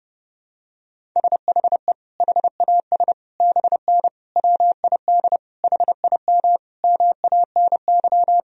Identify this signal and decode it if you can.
Morse code, heard as a keyed tone, about 32 wpm, 700 Hz. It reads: SHE HAS BN WID HIM MANY
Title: Sentences from Top 200 Words and Abbreviations 30wpm